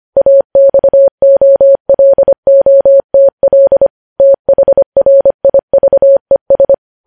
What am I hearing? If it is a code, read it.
Morse code, heard as a keyed tone, about 25 wpm, 550 Hz. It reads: AXOLOTL THRIVES